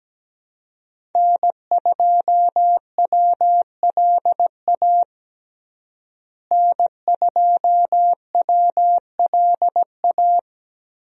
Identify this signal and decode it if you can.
Morse code, heard as a keyed tone, about 17 wpm, 700 Hz. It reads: N2WLA N2WLA
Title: USA Call Signs - 5 Characters - Repeated 2x - 17wpm